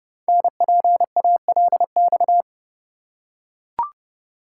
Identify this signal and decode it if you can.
Morse code, heard as a keyed tone, about 29 wpm, 700 Hz. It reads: NPALX
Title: Code Groups - 5 Letter:Numbers - 30wpm